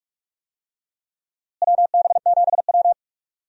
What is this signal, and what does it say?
Morse code, heard as a keyed tone, about 41 wpm, 700 Hz. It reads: WB7W
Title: USA Call Signs - 4 Characters - Repeated 3x - 45wpm